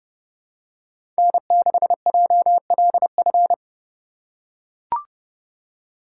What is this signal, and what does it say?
Morse code, heard as a keyed tone, about 30 wpm, 700 Hz. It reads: N6JLF
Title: USA Call Signs - 5 Characters - 2x Character Spacing - 30wpm